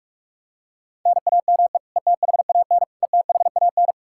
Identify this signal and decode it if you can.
Morse code, heard as a keyed tone, about 45 wpm, 700 Hz. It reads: NAME ETHAN ETHAN
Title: Practice QSO - Exchange 1 - Beginning - 45wpm